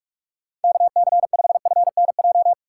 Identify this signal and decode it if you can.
Morse code, heard as a keyed tone, about 42 wpm, 700 Hz. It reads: KC5FNJ
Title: USA Call Signs - 6 Characters - 3x Character Spacing - 45wpm